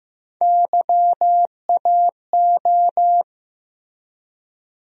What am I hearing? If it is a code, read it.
Morse code, heard as a keyed tone, about 15 wpm, 700 Hz. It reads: YAO
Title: USA Call Sign Postfixes - 3 Letters 15wpm